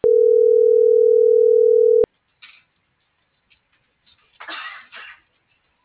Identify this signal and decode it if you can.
Ringback tone (North America), 440 + 480 Hz.